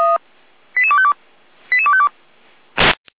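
Five-tone selective call (ZVEI), heard as quick runs of five tones, tones of about 70 ms. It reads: 88252 88252
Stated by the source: ZVEI five-tone selective call received by SDR, from audio server of Feuerwehr Orlamünde.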